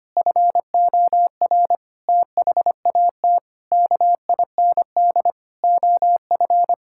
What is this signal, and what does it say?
Morse code, heard as a keyed tone, about 25 wpm, 700 Hz. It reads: FOR THAT KIND OF
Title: Sentences from Top 1000 Words - Review Part 6 - 25wpm